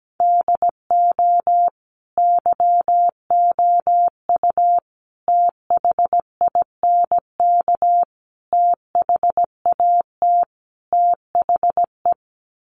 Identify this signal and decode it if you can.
Morse code, heard as a keyed tone, about 17 wpm, 700 Hz. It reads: DO YOU THINK THAT THE
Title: Sets of 5 Words Encore 17wpm